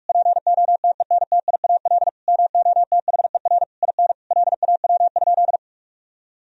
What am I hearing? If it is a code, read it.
Morse code, heard as a keyed tone, about 46 wpm, 700 Hz. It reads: POTENTIAL MOTHER IN LAW?
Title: Sentences from Top 1000 Words - Review Part 4 - 45wpm